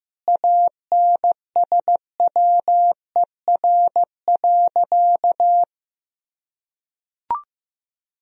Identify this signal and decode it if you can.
Morse code, heard as a keyed tone, about 15 wpm, 700 Hz. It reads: ANSWER.
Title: Sentences from Top 500 Words and Abbreviations 15wpm